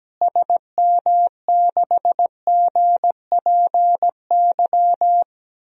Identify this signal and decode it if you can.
Morse code, heard as a keyed tone, about 17 wpm, 700 Hz. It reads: SM6GPY
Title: DXCC-100 Call Signs - 6 Characters - Repeated 3x - 17wpm